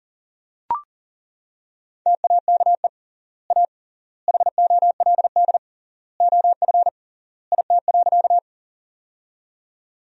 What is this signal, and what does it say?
Morse code, heard as a keyed tone, about 40 wpm, 700 Hz. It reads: TAKE A HOLD OF IT.
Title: 3x Word-Spacing - Sentences from Top 400 Words - 40wpm